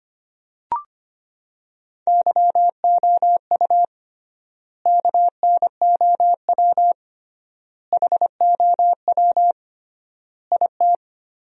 Morse code, heard as a keyed tone, about 25 wpm, 700 Hz. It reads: YOU KNOW HOW IT
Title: 3x Word-Spacing - Sentences from Top 100 Words - 25wpm